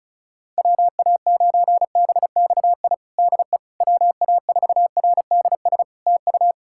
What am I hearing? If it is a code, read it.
Morse code, heard as a keyed tone, about 34 wpm, 700 Hz. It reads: WA9BXI DE WA4RDS TU
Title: Practice QSO - Exchange 4 - Ending - 35wpm